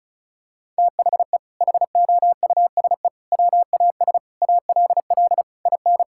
Morse code, heard as a keyed tone, about 34 wpm, 700 Hz. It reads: THE HOUSE WAS ALL IN
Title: Sentences from Top 1600 Words 35wpm